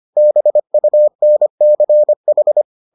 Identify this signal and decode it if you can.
Morse code, heard as a keyed tone, about 25 wpm, 600 Hz. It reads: BUNCH